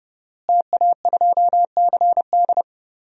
Morse code, heard as a keyed tone, about 30 wpm, 700 Hz. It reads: TA2CD
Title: DXCC-100 Call Signs - 5 Characters - Repeated 2x - 30wpm